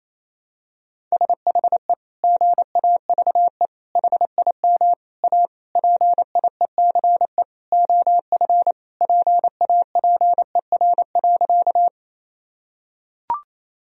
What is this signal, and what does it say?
Morse code, heard as a keyed tone, about 28 wpm, 700 Hz. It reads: SHE GAVE HIM A PIECE OF PAPER.
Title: Sentences from Top 1000 Words - Review Part 2 - 28wpm